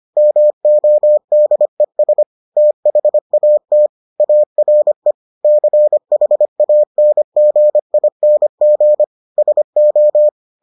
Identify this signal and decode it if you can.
Morse code, heard as a keyed tone, about 25 wpm, 600 Hz. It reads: MODES THAT ARE CHANGING SO